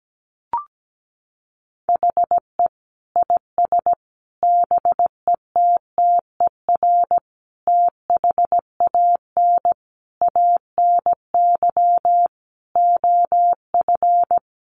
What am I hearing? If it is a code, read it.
Morse code, heard as a keyed tone, about 17 wpm, 700 Hz. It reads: HE IS BETTER THAN ANY OF
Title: Sentences from Top 400 Words 17wpm